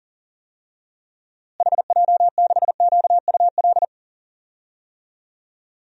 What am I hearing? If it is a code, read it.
Morse code, heard as a keyed tone, about 40 wpm, 700 Hz. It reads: HJ6QUL
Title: DXCC-100 Call Signs - 6 Characters - Repeated 3x - 40wpm